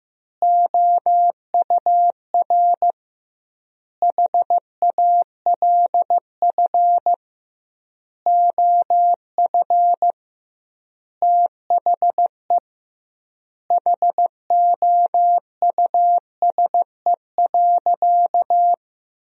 Morse code, heard as a keyed tone, about 15 wpm, 700 Hz. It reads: OUR HALF OF THE HOUSE.
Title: 2x Word-Spacing - Sentences from Top 400 Words - 15wpm